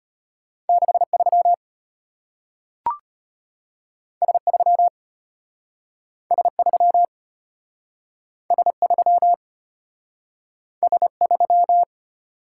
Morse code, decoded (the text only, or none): 63 S3 S3 S3 S3